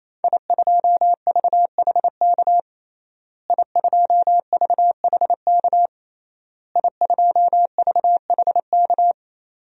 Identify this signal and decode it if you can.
Morse code, heard as a keyed tone, about 28 wpm, 700 Hz. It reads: I2VHK I2VHK I2VHK
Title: DXCC-100 Call Signs - 5 Characters - Repeated 3x - 28wpm